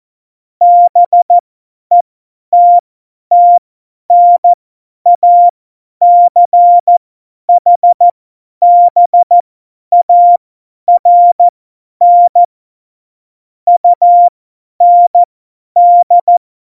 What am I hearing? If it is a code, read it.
Morse code, heard as a keyed tone, about 14 wpm, spaced as for 10 wpm, 700 Hz. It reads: BETTNACHBARN UND
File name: Morsekurs / Witze / Witz 15. September 2020, 50 F70